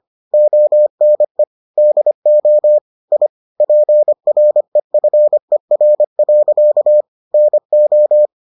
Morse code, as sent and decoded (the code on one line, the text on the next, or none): --- -. / -.. --- / .. / .--. .-. ..-. .-. .-.-.- / -. ---
ONE DO I PREFER. NO